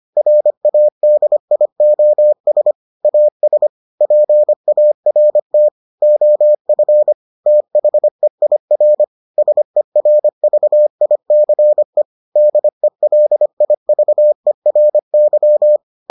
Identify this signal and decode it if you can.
Morse code, heard as a keyed tone, about 25 wpm, 600 Hz. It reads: RADIOS AS PART OF THEIR SERVICE DELIVERY